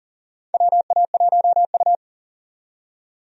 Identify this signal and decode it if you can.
Morse code, heard as a keyed tone, about 40 wpm, 690 Hz. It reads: WA1U
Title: USA Call Signs - 4 Characters - 40wpm